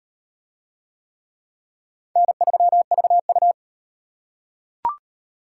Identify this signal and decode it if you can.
Morse code, heard as a keyed tone, about 38 wpm, 700 Hz. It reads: N3VU